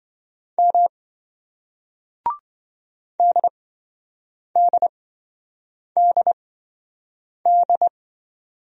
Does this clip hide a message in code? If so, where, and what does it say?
Yes: Morse code from 0.6 to 7.9 s, heard as a keyed tone, about 26 wpm, spaced as for 7 wpm, 700 Hz. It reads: M DDDD